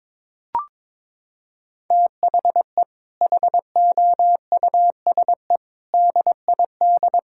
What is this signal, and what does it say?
Morse code, heard as a keyed tone, about 21 wpm, 700 Hz. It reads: THE HOUSE DID
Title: Sentences from Top 1600 Words 22wpm